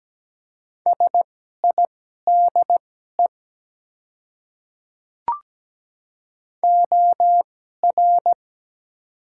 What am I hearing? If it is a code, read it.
Morse code, heard as a keyed tone, about 17 wpm, spaced as for 12 wpm, 700 Hz. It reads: SIDE OR
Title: 2x Character Spacing - Top 100 Words - 17wpm